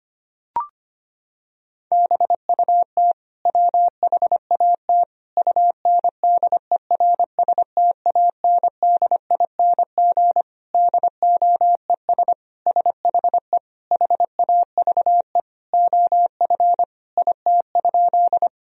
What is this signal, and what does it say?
Morse code, heard as a keyed tone, about 25 wpm, 700 Hz. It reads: BUT WHAT UNDERSTANDING DOES SHE HAVE OF IT?
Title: Sentences from Top 1000 Words - Review Part 3 - 25wpm